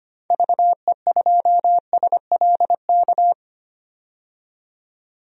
Morse code, heard as a keyed tone, about 25 wpm, 700 Hz. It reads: VE2SLK
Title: DXCC-100 Call Signs - 6 Characters - 25wpm